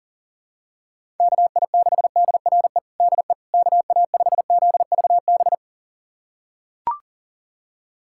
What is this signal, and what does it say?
Morse code, heard as a keyed tone, about 40 wpm, 700 Hz. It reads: KI6DRE DE KA5ZVB
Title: USA Callsign DE Callsign 40wpm